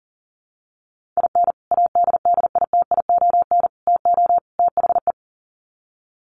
Morse code, heard as a keyed tone, about 40 wpm, 700 Hz. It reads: IN ADDITION TO THE
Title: Sets of 4 Words 40wpm